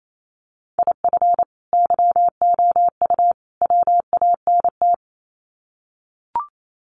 Morse code, heard as a keyed tone, about 28 wpm, 700 Hz. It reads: IF YOU WANT
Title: Sets of 3 Words 28wpm